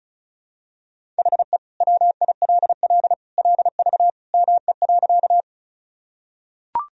Morse code, heard as a keyed tone, about 34 wpm, 700 Hz. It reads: HE WILL LV ME.
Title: Sentences from Top 400 Words and Abbreviations 35wpm